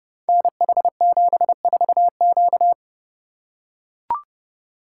Morse code, heard as a keyed tone, about 30 wpm, 700 Hz. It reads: NH74Q
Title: Code Groups - 5 Letter:Numbers - 30wpm